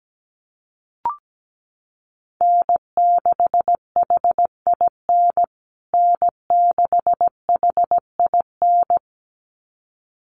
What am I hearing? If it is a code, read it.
Morse code, heard as a keyed tone, about 17 wpm, 700 Hz. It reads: N6HIN N6HIN
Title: USA Call Signs Repeated 2x 17wpm